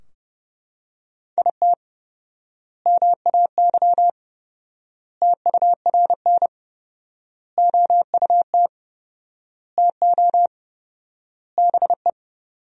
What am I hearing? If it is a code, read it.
Morse code, heard as a keyed tone, about 30 wpm, 700 Hz. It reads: IT MAY TURN OUT TO BE
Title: 4x Word-Spacing - Sentences from Top 400 Words - 30wpm